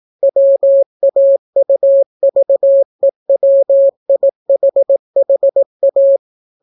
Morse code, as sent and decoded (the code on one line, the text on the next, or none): .-- .- ..- ...- . .-- .. .... .... .-
WAUVEWIHHA